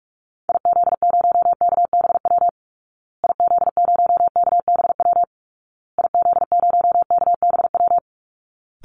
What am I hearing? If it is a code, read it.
Morse code, heard as a keyed tone, about 44 wpm, 700 Hz. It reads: IZ0KBW IZ0KBW IZ0KBW
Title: Call Signs Repeated 2x - 45wpm